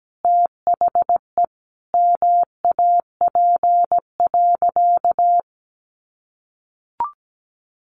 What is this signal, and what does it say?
Morse code, heard as a keyed tone, about 17 wpm, 700 Hz. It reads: THE MAP.
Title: Sentences from Top 400 Words 17wpm